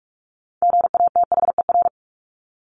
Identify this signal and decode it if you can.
Morse code, heard as a keyed tone, about 44 wpm, 700 Hz. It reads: GATHER
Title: Top 1501-1750 Words 45wpm